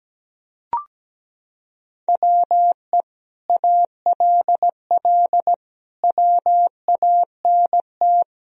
Morse code, heard as a keyed tone, about 17 wpm, 700 Hz. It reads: WE ALL WANT
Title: Sentences from Top 1000 Words - Review Part 4 - 17wpm